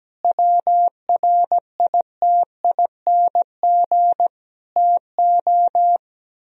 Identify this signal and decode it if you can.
Morse code, heard as a keyed tone, about 17 wpm, 700 Hz. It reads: WRITING TO
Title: Sentences from Top 1000 Words - Review Part 4 - 17wpm